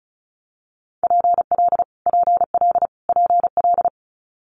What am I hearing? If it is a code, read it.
Morse code, heard as a keyed tone, about 36 wpm, 700 Hz. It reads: PL PL PL
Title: Multi-letter Phonemes Repeated 3x 35wpm